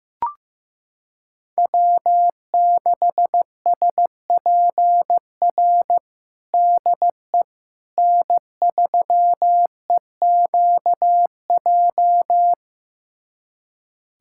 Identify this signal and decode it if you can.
Morse code, heard as a keyed tone, about 15 wpm, 700 Hz. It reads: W6SPR DE N3EQJ
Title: USA Callsign DE Callsign - 5 Characters - 15wpm